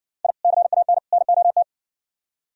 Morse code, heard as a keyed tone, about 59 wpm, 700 Hz. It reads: I CAN NOT